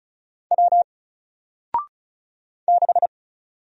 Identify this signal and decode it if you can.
Morse code, heard as a keyed tone, about 36 wpm, 700 Hz. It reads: W 6
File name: Single-Letter-Number-35wpm